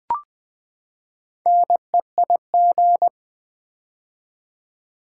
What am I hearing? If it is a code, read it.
Morse code, heard as a keyed tone, about 20 wpm, 700 Hz. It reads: NEIG
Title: Build A Word - 4 Letters At A Time - 11-Letter Words - 20wpm